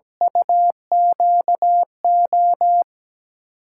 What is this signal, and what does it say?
Morse code, heard as a keyed tone, about 17 wpm, 700 Hz. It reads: UQO
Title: USA Call Signs - 8x Character Spacing - 17wpm